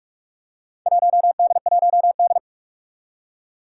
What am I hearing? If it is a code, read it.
Morse code, heard as a keyed tone, about 43 wpm, 700 Hz. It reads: 1D1D
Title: Four Challenging Letter-Number Combinations 45wpm